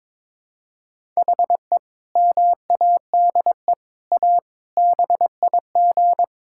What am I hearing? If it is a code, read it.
Morse code, heard as a keyed tone, about 21 wpm, 700 Hz. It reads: HE MADE A BIG
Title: Sentences from Top 1000 Words - Review Part 6 - 22wpm